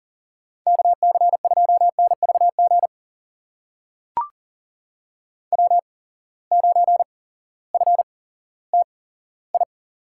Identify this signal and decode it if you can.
Morse code, heard as a keyed tone, about 37 wpm, spaced as for 18 wpm, 700 Hz. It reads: KC2NVG W9FTI
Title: USA Call Signs - 8x Character Spacing - 40wpm